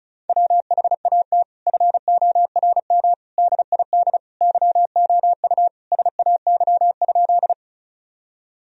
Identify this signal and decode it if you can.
Morse code, heard as a keyed tone, about 34 wpm, 700 Hz. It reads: WHAT FORM DID YOU SAY?
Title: Sentences from Top 1000 Words - Review Part 1 - 35wpm